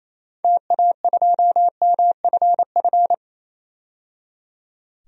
Morse code, heard as a keyed tone, about 28 wpm, 700 Hz. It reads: TA2MFF
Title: DXCC-100 Call Signs - 6 Characters - 28wpm